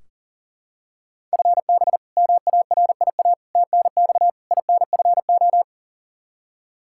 Morse code, heard as a keyed tone, about 37 wpm, 700 Hz. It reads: FB MARIA TNX INFO
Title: K1USN SST QSO Practice 40wpm